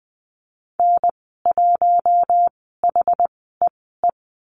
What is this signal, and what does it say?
Morse code, heard as a keyed tone, about 20 wpm, spaced as for 15 wpm, 700 Hz. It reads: N1HEE
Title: 062 Lesson 11g - Letter h - Call Signs 20wpm